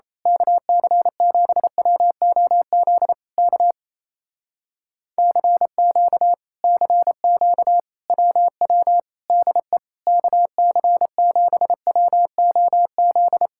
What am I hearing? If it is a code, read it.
Morse code, heard as a keyed tone, about 27 wpm, 700 Hz. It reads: KC7WOZ K CQ CQ WW DE KC7WOZ
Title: Speed-Racing Common QSO Phrases - 28wpm